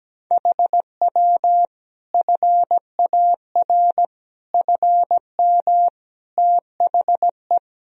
Morse code, heard as a keyed tone, about 17 wpm, 700 Hz. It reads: HW FAR FM THE
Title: Sentences from Top 400 Words and Abbreviations 17wpm